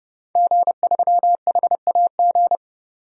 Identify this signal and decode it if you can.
Morse code, heard as a keyed tone, about 32 wpm, 700 Hz. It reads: G3HAG
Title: DXCC-100 Call Signs - 5 Characters - 30wpm